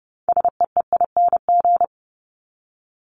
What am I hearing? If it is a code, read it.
Morse code, heard as a keyed tone, about 30 wpm, 700 Hz. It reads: SEEING
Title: ING Suffix - Top 100 Words with Base Words 30wpm